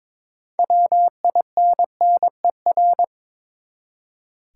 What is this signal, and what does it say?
Morse code, heard as a keyed tone, about 21 wpm, 700 Hz. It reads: WINNER
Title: Six Letter Words - 22wpm